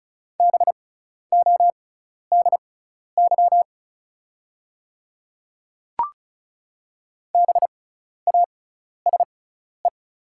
Morse code, heard as a keyed tone, about 36 wpm, spaced as for 12 wpm, 690 Hz. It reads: BODY BASE
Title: 6x Character Spacing - Top 500 Words - 35wpm